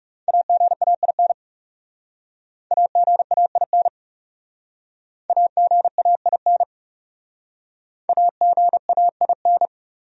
Morse code, heard as a keyed tone, about 36 wpm, 700 Hz. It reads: AGAIN AGAIN AGAIN AGAIN
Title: Speed-Racing Top 200 Words 30wpm